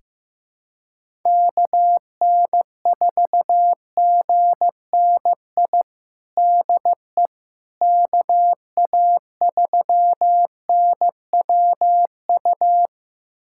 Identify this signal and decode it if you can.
Morse code, heard as a keyed tone, about 15 wpm, 700 Hz. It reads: KN4GNI DE KA3NWU
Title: USA Callsign DE Callsign - 6 Characters - 15wpm